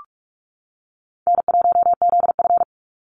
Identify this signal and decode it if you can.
Morse code, heard as a keyed tone, about 46 wpm, 700 Hz. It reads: N1ZF
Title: NA SOTA Chaser Callsigns 45wpm